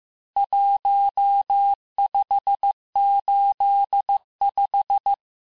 Morse code, heard as a keyed tone, about 15 wpm, 800 Hz. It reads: 1585
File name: cours-du-lundi files / cours-1585-texte-ponctuations-vitesse-14mpm-puis-19mpm